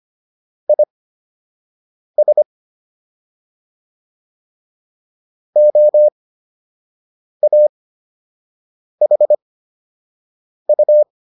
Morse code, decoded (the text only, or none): IS OAHU